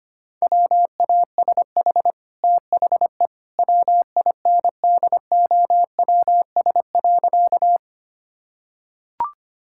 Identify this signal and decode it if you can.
Morse code, heard as a keyed tone, about 25 wpm, 700 Hz. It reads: WASH THE WINDOWS.
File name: Easy Sentences with Hints - 25wpm